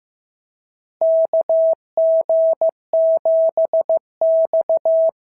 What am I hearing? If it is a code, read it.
Morse code, heard as a keyed tone, about 14 wpm, 650 Hz. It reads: KG7X
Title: SOTA QSO Practice 15wpm